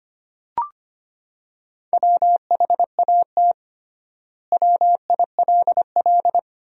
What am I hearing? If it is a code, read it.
Morse code, heard as a keyed tone, about 25 wpm, 700 Hz. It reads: WHAT WILL
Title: 3x Word-Spacing - Sentences from Top 200 Words - 25wpm